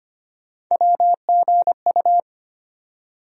Morse code, heard as a keyed tone, about 25 wpm, 700 Hz. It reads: WGU
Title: USA Call Sign Postfixes - 3 Letters 25wpm